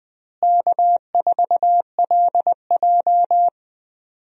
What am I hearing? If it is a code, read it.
Morse code, heard as a keyed tone, about 21 wpm, 700 Hz. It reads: K4LJ
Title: USA Call Signs - 4 Characters - Repeated 3x - 20wpm